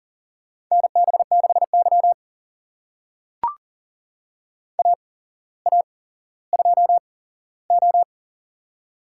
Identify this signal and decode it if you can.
Morse code, heard as a keyed tone, about 37 wpm, spaced as for 17 wpm, 700 Hz. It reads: NB6Y AA2O
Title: USA Call Signs - 4 Characters - 8x Character Spacing - 40wpm